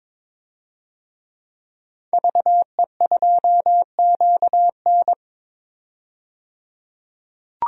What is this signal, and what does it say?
Morse code, heard as a keyed tone, about 22 wpm, 700 Hz. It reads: VE2QN
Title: DXCC-100 Call Signs - 5 Characters - 22wpm